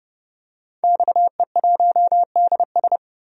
Morse code, heard as a keyed tone, about 31 wpm, 700 Hz. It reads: XE1DS
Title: DXCC-100 Call Signs - 5 Characters - 30wpm